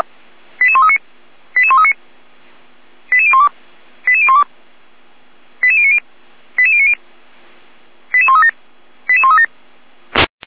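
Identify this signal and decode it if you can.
Five-tone selective call (ZVEI), heard as quick runs of five tones, tones of about 70 ms. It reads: 88139 88139 88012 88012 88090 88090 88137 88137
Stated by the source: ZVEI five-tone selective call received on SDR.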